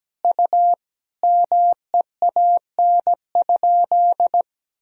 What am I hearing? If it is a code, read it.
Morse code, heard as a keyed tone, about 17 wpm, 700 Hz. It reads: U MEAN?
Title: Sentences from Top 200 Words and Abbreviations 17wpm